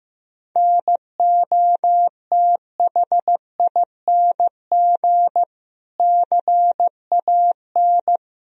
Morse code, heard as a keyed tone, about 15 wpm, 700 Hz. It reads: NOTHING CAN
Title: Sentences from Top 1000 Words - Review Part 4 - 15wpm